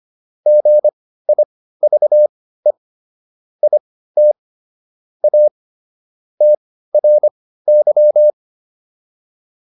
Morse code, heard as a keyed tone, about 25 wpm, spaced as for 15 wpm, 600 Hz. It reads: GIVE IT A TRY